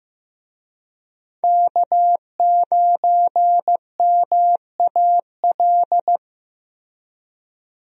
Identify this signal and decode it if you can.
Morse code, heard as a keyed tone, about 15 wpm, 700 Hz. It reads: K9MAL
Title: USA Call Signs - 5 Characters - Repeated 3x - 15wpm